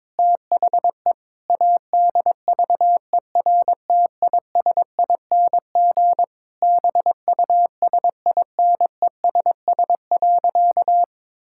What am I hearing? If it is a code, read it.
Morse code, heard as a keyed tone, about 22 wpm, 700 Hz. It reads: THE ADVERTISING BUSINESS.
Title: Sentences from Top 1000 Words - Review Part 6 - 22wpm